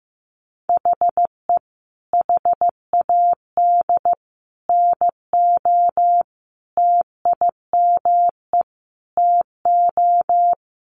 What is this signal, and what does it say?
Morse code, heard as a keyed tone, about 15 wpm, 700 Hz. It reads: HE HAD NO TIME TO
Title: Sentences from Top 700 Words 15wpm